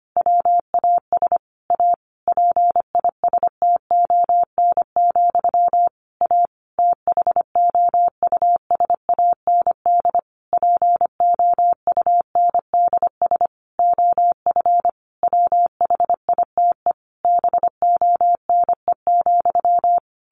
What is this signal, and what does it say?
Morse code, heard as a keyed tone, about 25 wpm, 700 Hz. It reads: WAS A PISTON, A THOUSAND POUNDS OF WHITE BONE,
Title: A Sound Of Thunder 25wpm